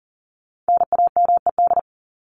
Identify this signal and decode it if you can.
Morse code, heard as a keyed tone, about 40 wpm, 700 Hz. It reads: NAMED
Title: ED Suffix - Top 100 Words with Base Words 40wpm